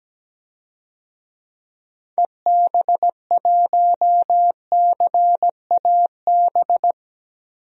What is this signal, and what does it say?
Morse code, heard as a keyed tone, about 17 wpm, 700 Hz. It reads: EB1CAB
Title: DXCC-100 Call Signs - 6 Characters - 17wpm